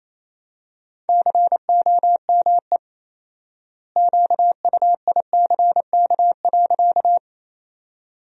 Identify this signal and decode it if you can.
Morse code, heard as a keyed tone, about 28 wpm, 700 Hz. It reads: COME QUICK.
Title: 4x Word-Spacing - Sentences from Top 500 Words - 28wpm